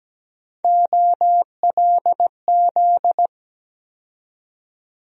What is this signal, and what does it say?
Morse code, heard as a keyed tone, about 17 wpm, 700 Hz. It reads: OLZ
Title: USA Call Sign Postfixes - 3 Letters 17wpm